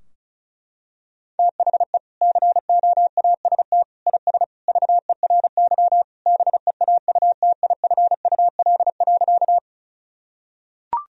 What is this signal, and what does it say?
Morse code, heard as a keyed tone, about 34 wpm, 700 Hz. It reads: THE COAST IS VERY BEAUTIFUL.
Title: Sentences from Top 1000 Words - Review Part 4 - 35wpm